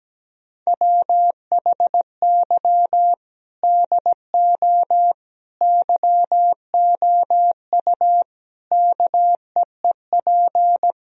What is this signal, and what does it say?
Morse code, heard as a keyed tone, about 17 wpm, 700 Hz. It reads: WHY DO YOU KEEP
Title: Sentences from Top 1000 Words - Review Part 3 - 17wpm